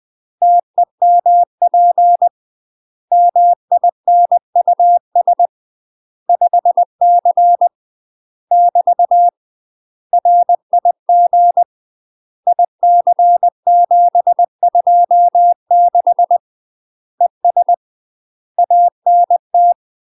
Morse code, heard as a keyed tone, about 20 wpm, 700 Hz. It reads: TEMP MINUS 5C <BT> RIG IC726 ES ANT
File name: UK MASTER QSOs TRAINING / 20WPM / UK MASTER QSOs 0058